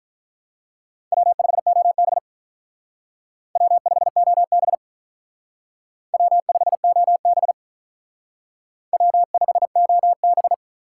Morse code, heard as a keyed tone, about 44 wpm, 700 Hz. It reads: W5OB W5OB W5OB W5OB